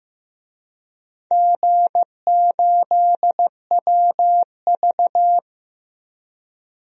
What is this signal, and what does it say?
Morse code, heard as a keyed tone, about 15 wpm, 700 Hz. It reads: G8WV
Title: DXCC-100 Call Signs - 4 Characters - Repeated 3x - 15wpm